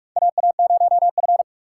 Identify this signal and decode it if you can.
Morse code, heard as a keyed tone, about 46 wpm, 700 Hz. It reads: AA0F
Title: USA Call Signs - 3x Character Spacing - 45wpm